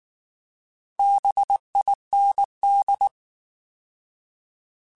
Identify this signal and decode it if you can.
Morse code, heard as a keyed tone, about 19 wpm, 800 Hz. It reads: BIND
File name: Alphabetical / Alphabetical mp3s / Bs @18